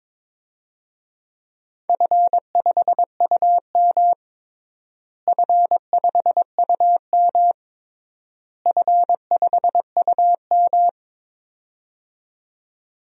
Morse code, heard as a keyed tone, about 22 wpm, 700 Hz. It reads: F5UM F5UM F5UM
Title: DXCC-100 Call Signs - 4 Characters - Repeated 3x - 22wpm